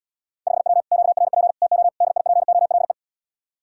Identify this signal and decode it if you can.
Morse code, heard as a keyed tone, about 80 wpm, 700 Hz. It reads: 5W QRP TO DIPOLE